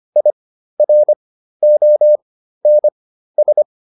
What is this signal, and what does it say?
Morse code, heard as a keyed tone, about 25 wpm, spaced as for 13 wpm, 600 Hz. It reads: IRONS
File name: IR5 13